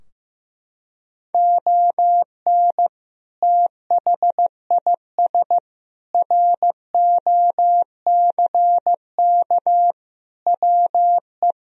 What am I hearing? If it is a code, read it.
Morse code, heard as a keyed tone, about 15 wpm, 700 Hz. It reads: ON THIS ROCK WE
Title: Sentences from Top 1000 Words - Review Part 2 - 15wpm